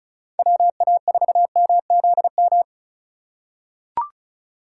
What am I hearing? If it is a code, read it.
Morse code, heard as a keyed tone, about 36 wpm, 700 Hz. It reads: WA4MZM
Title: USA Call Signs - 6 Characters - 3x Character Spacing - 35wpm